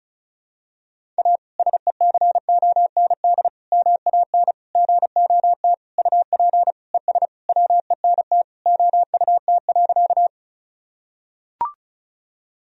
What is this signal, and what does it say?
Morse code, heard as a keyed tone, about 36 wpm, 700 Hz. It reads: A SECOND MAN GOT UP ES WENT OUT.
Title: Sentences from Top 300 Words and Abbreviations 35wpm